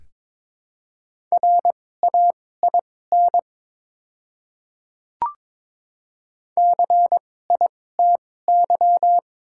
Morse code, heard as a keyed tone, about 22 wpm, spaced as for 16 wpm, 690 Hz. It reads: RAIN CITY
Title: 2x Character Spacing - Top 500 Words - 22wpm